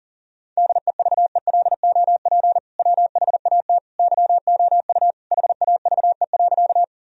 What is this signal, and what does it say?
Morse code, heard as a keyed tone, about 37 wpm, 700 Hz. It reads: DEVELOP WHAT YOU HAVE.
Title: Sentences from Top 1000 Words - Review Part 2 - 40wpm